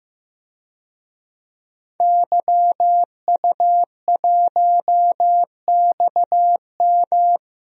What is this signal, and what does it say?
Morse code, heard as a keyed tone, about 15 wpm, 700 Hz. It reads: YU1XM
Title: DXCC-100 Call Signs - 5 Characters - Repeated 3x - 15wpm